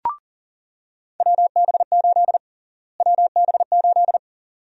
Morse code, heard as a keyed tone, about 40 wpm, 700 Hz. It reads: WB8 WB8
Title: USA Call Sign Prefixes - Repeated 2x - 40wpm